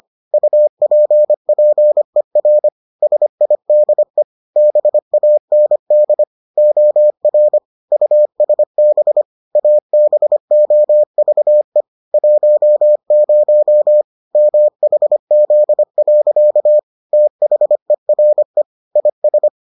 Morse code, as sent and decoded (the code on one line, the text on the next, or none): ..- .--. .--. .-. / ... .. -.. / -... .- -. -.. / --- .-. / ..- ... -... / .- -... --- ...- . / .---- ----- / -- .... --.. .-.-.- / - .... .-. / .. ...
UPPER SIDE BAND OR USB ABOVE 10 MHZ. THERE IS